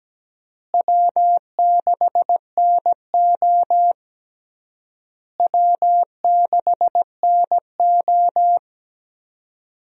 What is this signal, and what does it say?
Morse code, heard as a keyed tone, about 17 wpm, 700 Hz. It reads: W6NO W6NO
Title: USA Call Signs - 4 Characters - Repeated 3x - 17wpm